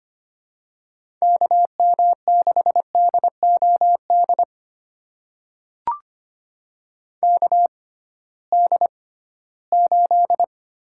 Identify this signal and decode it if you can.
Morse code, heard as a keyed tone, about 25 wpm, spaced as for 16 wpm, 700 Hz. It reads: KM6DOD KD8
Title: USA Call Signs - 6 Characters - 6x Character Spacing - 25wpm